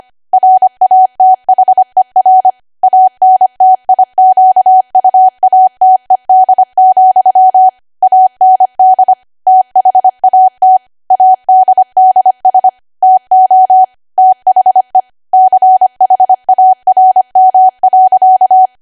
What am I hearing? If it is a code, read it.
Morse code, heard as a keyed tone, about 25 wpm, 750 Hz. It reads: RATHER ANTIQUATED, AND THAT ADDS TO THE CHARM.
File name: cwa / W1AW-Level3 / 25WPM